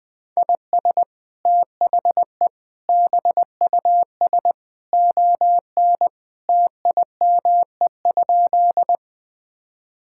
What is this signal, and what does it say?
Morse code, heard as a keyed tone, about 20 wpm, 700 Hz. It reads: IS THE BUS ON TIME?
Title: Sentences from Top 1000 Words - Review Part 6 - 20wpm